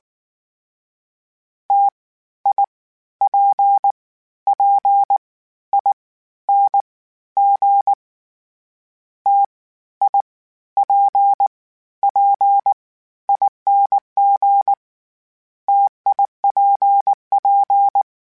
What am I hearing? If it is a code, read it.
Morse code, heard as a keyed tone, about 19 wpm, 800 Hz. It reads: T I P P I N G T I P P ING TIPP
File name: Variable Spacing / Variable Spacing-mp3s / T AOEs @18